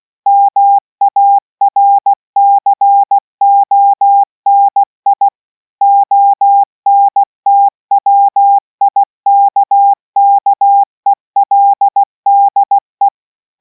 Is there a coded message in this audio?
Yes: Morse code from 0.3 to 13.1 s, heard as a keyed tone, about 16 wpm, 800 Hz. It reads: MARCONI ONTWIKKELDE